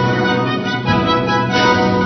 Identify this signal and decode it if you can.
There is no code.